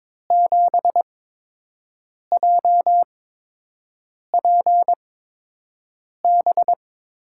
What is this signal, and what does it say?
Morse code, heard as a keyed tone, about 22 wpm, spaced as for 6 wpm, 700 Hz. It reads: 7JPB
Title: USA Call Signs - 6 Characters - 8x Character Spacing - 22wpm